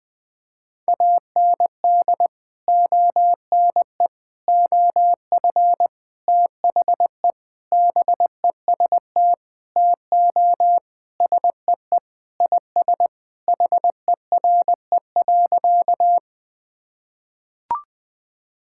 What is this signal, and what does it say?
Morse code, heard as a keyed tone, about 20 wpm, 700 Hz. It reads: AND ONE OF THE BEST TO SEE IS HERE.
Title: Sentences with 4-Word Phrases - Spoken Beginning - 20wpm